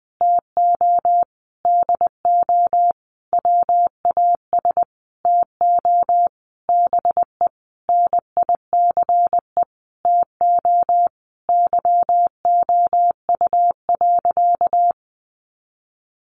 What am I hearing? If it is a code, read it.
Morse code, heard as a keyed tone, about 20 wpm, 700 Hz. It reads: TO DO WAS TO BE NICE TO YOU.
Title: Sentences from Top 1100 Words 20wpm